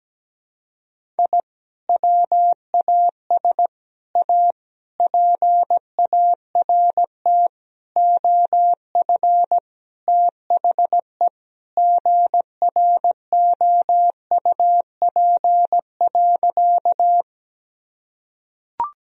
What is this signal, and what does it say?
Morse code, heard as a keyed tone, about 17 wpm, 700 Hz. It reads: I WAS A PART OF THE GROUP.
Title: Sentences from Top 1000 Words - Review Part 1 - 17wpm